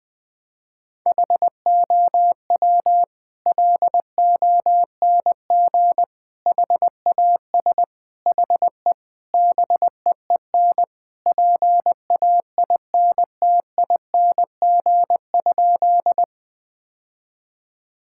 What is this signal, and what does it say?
Morse code, heard as a keyed tone, about 21 wpm, 700 Hz. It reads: HOW LONG HAS HE BEEN PAINTING?
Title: Sentences from Top 1000 Words - Review Part 6 - 20wpm